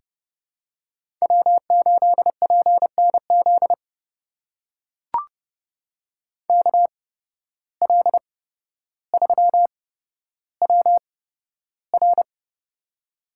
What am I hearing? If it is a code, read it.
Morse code, heard as a keyed tone, about 30 wpm, spaced as for 13 wpm, 700 Hz. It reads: W8PNZ KL3WR
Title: USA Call Signs - 5 Characters - 8x Character Spacing - 30wpm